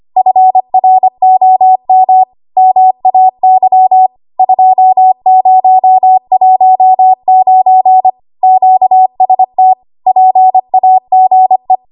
Morse code, heard as a keyed tone, about 25 wpm, 750 Hz. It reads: FROM MAY 2019 QST PAGE
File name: Archive / 25 WPM / 25WPM